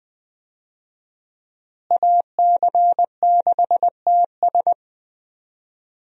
Morse code, heard as a keyed tone, about 20 wpm, 700 Hz. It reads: AC6TS
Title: USA Call Signs - 5 Characters - Repeated 2x - 20wpm